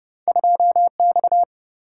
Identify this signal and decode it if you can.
Morse code, heard as a keyed tone, about 31 wpm, 700 Hz. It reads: 2 X
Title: Single-Letter Single-Number 30wpm